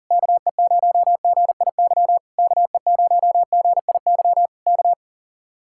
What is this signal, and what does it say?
Morse code, heard as a keyed tone, about 40 wpm, 700 Hz. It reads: KE0GIY KE0GIY K